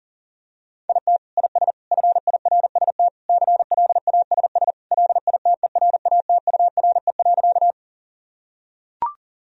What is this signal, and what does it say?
Morse code, heard as a keyed tone, about 40 wpm, 700 Hz. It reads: IT IS FIRST CLASS LITERATURE.
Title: Sentences from Top 1000 Words - Review Part 3 - 40wpm